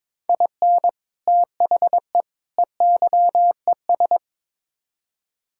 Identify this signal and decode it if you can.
Morse code, heard as a keyed tone, about 22 wpm, 700 Hz. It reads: IN THE EYES
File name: practice / 22wpm / Build A Phrase - 5-Word Phrases - 22wpm